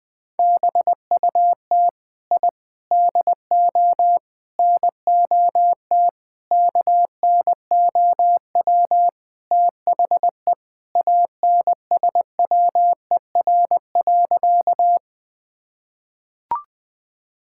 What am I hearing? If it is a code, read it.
Morse code, heard as a keyed tone, about 20 wpm, 700 Hz. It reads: BUT I DO NOT KNOW THE ANSWER.
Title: Sentences with 5-Word Phrases - Spoken Beginning - 20wpm